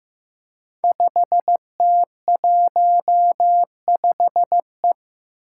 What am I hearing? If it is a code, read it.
Morse code, heard as a keyed tone, about 15 wpm, 700 Hz. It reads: 5T15E
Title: Five Challenging Letter-Number Combinations 15wpm